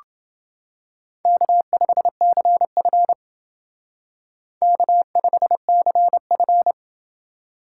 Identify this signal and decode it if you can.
Morse code, heard as a keyed tone, about 28 wpm, 700 Hz. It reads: K5CF K5CF